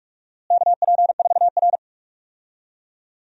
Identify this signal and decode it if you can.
Morse code, heard as a keyed tone, about 44 wpm, 690 Hz. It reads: KP4R
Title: DXCC-100 Call Signs - 4 Characters - Repeated 3x - 45wpm